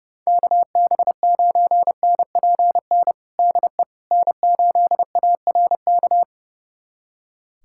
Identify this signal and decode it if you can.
Morse code, heard as a keyed tone, about 30 wpm, 700 Hz. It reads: KB9NPN DE N8ARK
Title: USA Callsign DE Callsign 30wpm